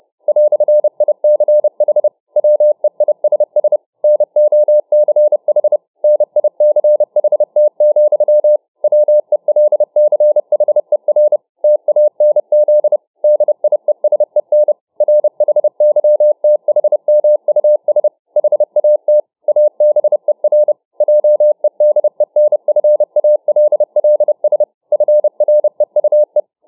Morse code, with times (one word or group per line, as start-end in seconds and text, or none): 0.3-2.1 "ICH
2.4-3.8 WEISS
4.0-5.8 NOCH
6.0-8.6 NICHT,
8.8-11.4 WELCHER
11.6-13.0 TANZ
13.2-14.7 DIESEN
15.0-18.1 RHYTHMUS
18.4-19.2 HAT
19.5-20.7 ABER
21.0-24.6 JEDENFALLS
24.9-26.4 FREUE